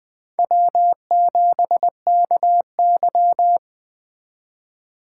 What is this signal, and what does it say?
Morse code, heard as a keyed tone, about 21 wpm, 700 Hz. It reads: W7KY